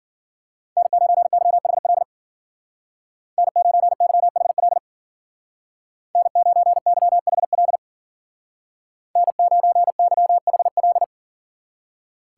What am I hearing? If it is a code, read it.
Morse code, heard as a keyed tone, about 49 wpm, 700 Hz. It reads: N9YHL N9YHL N9YHL N9YHL
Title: Speed Racing - USA Call Signs 40wpm